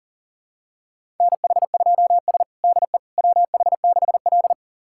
Morse code, heard as a keyed tone, about 37 wpm, 700 Hz. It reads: NH2S DE WH6L
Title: USA Callsign DE Callsign - 4 Characters - 40wpm